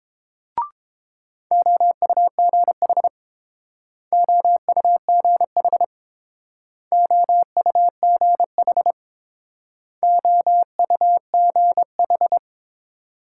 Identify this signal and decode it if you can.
Morse code, heard as a keyed tone, about 30 wpm, 700 Hz. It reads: OUGH OUGH OUGH OUGH